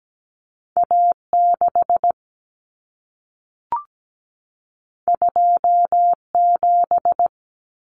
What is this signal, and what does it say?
Morse code, heard as a keyed tone, about 17 wpm, 700 Hz. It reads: A6 27
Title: Common Cut Numbers - 2-Digit 17wpm